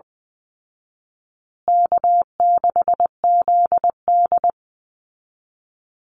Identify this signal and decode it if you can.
Morse code, heard as a keyed tone, about 20 wpm, 700 Hz. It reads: K6ZD
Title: Easy Call Signs Repeated 3x - 20wpm